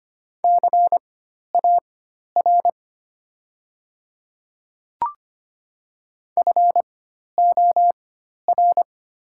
Morse code, heard as a keyed tone, about 25 wpm, spaced as for 12 wpm, 700 Hz. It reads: CAR FOR
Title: 4x Character Spacing - Top 300 Words - 25wpm